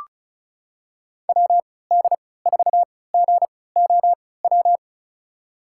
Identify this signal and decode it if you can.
Morse code, heard as a keyed tone, about 36 wpm, spaced as for 20 wpm, 700 Hz. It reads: WD4GOW
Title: USA Call Signs - 6 Characters - 3x Character Spacing - 35wpm